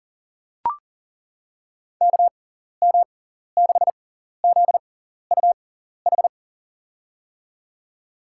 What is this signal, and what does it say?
Morse code, heard as a keyed tone, about 40 wpm, spaced as for 14 wpm, 700 Hz. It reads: KM6ZUH
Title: USA Call Signs - 6 Characters - 6x Character Spacing - 40wpm